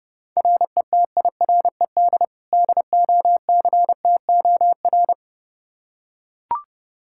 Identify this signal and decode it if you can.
Morse code, heard as a keyed tone, about 30 wpm, 700 Hz. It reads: RETIRED DOCTOR
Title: Practice QSO - Exchange 3 - More Info - 30wpm